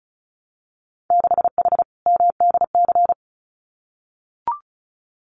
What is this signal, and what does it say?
Morse code, heard as a keyed tone, about 34 wpm, 700 Hz. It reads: 6H MDC
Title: Winter Field Day Exchanges Encore 35wpm